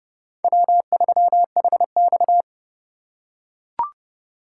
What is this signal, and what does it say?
Morse code, heard as a keyed tone, about 28 wpm, 700 Hz. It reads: W3HX